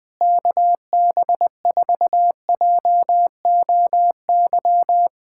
Morse code, heard as a keyed tone, about 20 wpm, 700 Hz. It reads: KB4JOY
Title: POTA QSO Practice - 20wpm